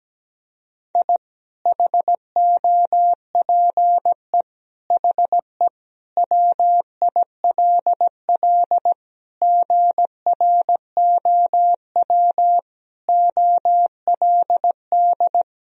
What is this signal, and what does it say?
Morse code, heard as a keyed tone, about 17 wpm, 700 Hz. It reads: I HOPE HE WILL GROW OLD
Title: Sentences from Top 1000 Words - Review Part 1 - 17wpm